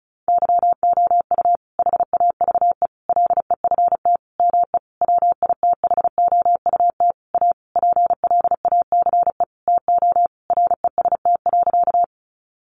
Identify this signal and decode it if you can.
Morse code, heard as a keyed tone, about 34 wpm, 700 Hz. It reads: YOU HAVE LEFT ME WITHOUT A PLACE TO REST.
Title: Sentences from Top 600 Words 35wpm